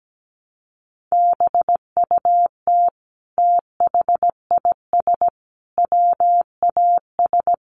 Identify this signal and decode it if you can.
Morse code, heard as a keyed tone, about 17 wpm, 700 Hz. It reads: BUT THIS WAS
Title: Sentences from Top 600 Words 17wpm